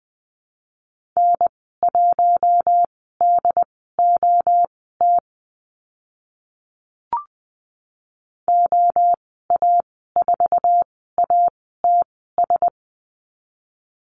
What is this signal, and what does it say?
Morse code, heard as a keyed tone, about 20 wpm, spaced as for 15 wpm, 700 Hz. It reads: N1DOT OA4ATS
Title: 082 Lesson 14g - Number 2 - Call Signs 20wpm